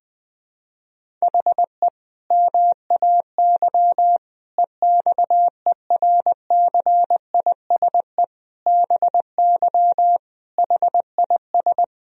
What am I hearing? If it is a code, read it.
Morse code, heard as a keyed tone, about 20 wpm, 700 Hz. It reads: HE MAY EXERCISE BY HIS
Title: Sentences from Top 1000 Words - Review Part 4 - 20wpm